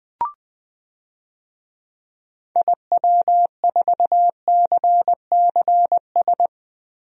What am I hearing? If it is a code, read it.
Morse code, heard as a keyed tone, about 20 wpm, 700 Hz. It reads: IW4CCS
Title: DXCC-100 Call Signs - 6 Characters - Repeated 3x - 20wpm